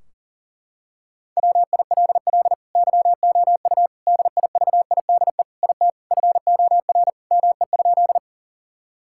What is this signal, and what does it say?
Morse code, heard as a keyed tone, about 40 wpm, 700 Hz. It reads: WILL YOU DIVIDE IT FOR ME?
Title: Sentences from Top 1600 Words 40wpm